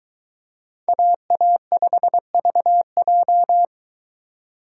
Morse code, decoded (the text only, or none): AA5VJ